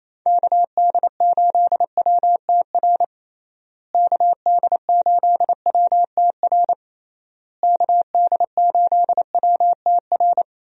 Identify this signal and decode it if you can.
Morse code, heard as a keyed tone, about 28 wpm, 700 Hz. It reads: KD8WTR KD8WTR KD8WTR
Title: USA Call Signs - 6 Characters - Repeated 3x - 28wpm